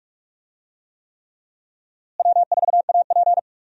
Morse code, heard as a keyed tone, about 41 wpm, 700 Hz. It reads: W4AP
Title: USA Call Signs - 4 Characters - Repeated 3x - 45wpm